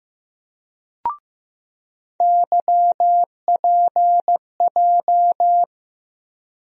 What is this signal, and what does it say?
Morse code, heard as a keyed tone, about 16 wpm, 700 Hz. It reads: YPJ
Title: USA Call Sign Postfixes - 3 Letters 15wpm